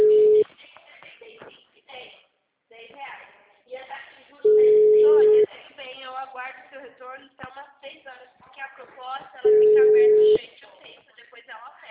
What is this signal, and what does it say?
Ringback tone (Europe, 425 Hz), with speech.